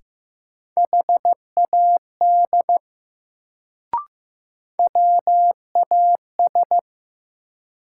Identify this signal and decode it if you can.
Morse code, heard as a keyed tone, about 15 wpm, 700 Hz. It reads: HAD WAS
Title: Sets of 2 Words - New Word Drill - 15wpm